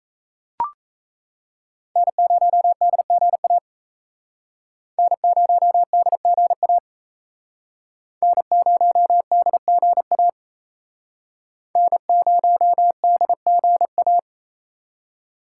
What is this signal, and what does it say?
Morse code, heard as a keyed tone, about 36 wpm, 700 Hz. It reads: N0DGA N0DGA N0DGA N0DGA